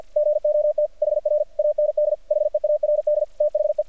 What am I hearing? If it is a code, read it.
Morse code, heard as a keyed tone, about 48 wpm, 600 Hz. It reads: GOT HR AND HEARD THE
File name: archived / CRSnet